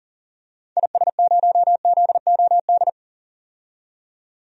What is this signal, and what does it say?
Morse code, heard as a keyed tone, about 40 wpm, 700 Hz. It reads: IS0ZOD
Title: DXCC-100 Call Signs - 6 Characters - 40wpm